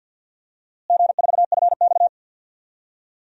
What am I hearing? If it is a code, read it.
Morse code, heard as a keyed tone, about 51 wpm, 700 Hz. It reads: G4FX
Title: DXCC-100 Call Signs - 4 Characters - Repeated 2x - 50wpm